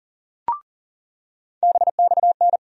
Morse code, heard as a keyed tone, about 40 wpm, 700 Hz. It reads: BXN
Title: USA Call Sign Postfixes - 3 Letters 40wpm